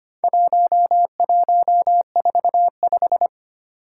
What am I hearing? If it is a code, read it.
Morse code, heard as a keyed tone, about 25 wpm, 700 Hz. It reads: 1145
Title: Four Challenging Letter-Number Combinations 25wpm